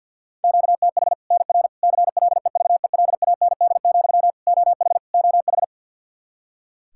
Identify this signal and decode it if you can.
Morse code, heard as a keyed tone, about 51 wpm, 700 Hz. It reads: QTH NR CLEVELAND, OH OH
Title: K1USN SST QSO Practice 50wpm